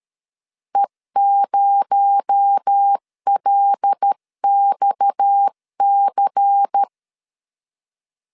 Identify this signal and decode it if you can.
Morse code, heard as a keyed tone, about 13 wpm, 800 Hz. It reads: E0LXC
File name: LEZIONI / 10 SIGNS / 10B INT 12wpm 2414